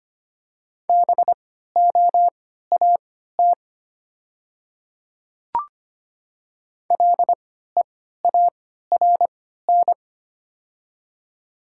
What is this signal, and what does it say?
Morse code, heard as a keyed tone, about 25 wpm, spaced as for 14 wpm, 700 Hz. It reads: BOAT LEARN
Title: 3x Character Spacing - Top 500 Words - 25wpm